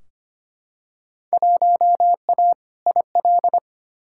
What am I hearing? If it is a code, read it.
Morse code, heard as a keyed tone, about 25 wpm, 700 Hz. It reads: 1A IL
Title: Field Day Exchanges Encore - 25wpm